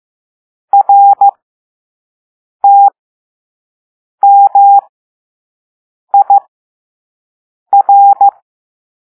Morse code, heard as a keyed tone, about 15 wpm, spaced as for 5 wpm, 800 Hz. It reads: RTMIR